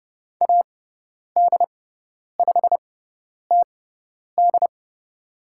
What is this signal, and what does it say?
Morse code, heard as a keyed tone, about 30 wpm, spaced as for 10 wpm, 700 Hz. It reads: AD5TD
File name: Lesson 16g - Letter c - Call Signs 30wpm